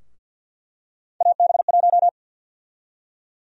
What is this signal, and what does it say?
Morse code, heard as a keyed tone, about 50 wpm, 700 Hz. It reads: AB1
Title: USA Call Sign Prefixes - 50wpm